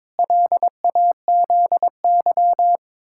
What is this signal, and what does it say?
Morse code, heard as a keyed tone, about 22 wpm, 700 Hz. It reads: LAZY
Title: Pangram Challenge v2 - 22wpm